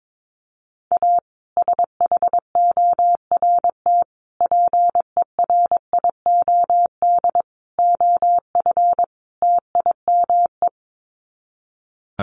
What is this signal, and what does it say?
Morse code, heard as a keyed tone, about 22 wpm, 700 Hz. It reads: A SHORT PERIOD OF TIME
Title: Sets of 5 Words Encore 22wpm